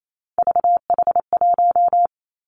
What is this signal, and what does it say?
Morse code, heard as a keyed tone, about 28 wpm, 700 Hz. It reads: VH1
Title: Three Challenging Letter-Number Combinations 28wpm